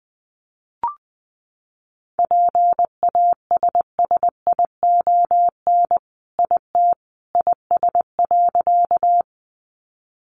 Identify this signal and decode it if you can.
Morse code, heard as a keyed tone, about 20 wpm, 700 Hz. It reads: PASSION IT IS.
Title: Sentences from Top 1300 Words 20wpm